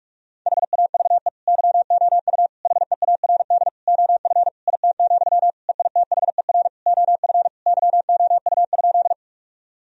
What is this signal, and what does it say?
Morse code, heard as a keyed tone, about 41 wpm, 700 Hz. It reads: HAVE YOU HEARD OF IT, EITHER OF YOU?
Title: Sentences from Top 1000 Words - Review Part 4 - 45wpm